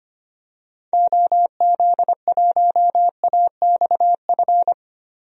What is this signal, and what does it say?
Morse code, heard as a keyed tone, about 25 wpm, 700 Hz. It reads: OZ1AXF
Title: DXCC-100 Call Signs - 6 Characters - 25wpm